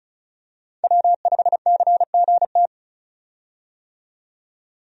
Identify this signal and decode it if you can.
Morse code, heard as a keyed tone, about 34 wpm, 700 Hz. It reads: W5CGT
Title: USA Call Signs - 5 Characters - 35wpm